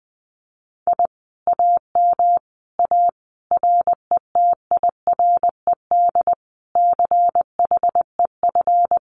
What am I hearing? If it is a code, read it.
Morse code, heard as a keyed tone, about 20 wpm, 700 Hz. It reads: I AM A RETIRED CHEF
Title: Common QSO Phrases Encore 20wpm